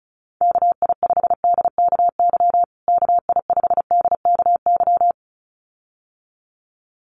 Morse code, heard as a keyed tone, about 34 wpm, 700 Hz. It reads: KI5DKY KI5DKY
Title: USA Call Signs Repeated 2x 35wpm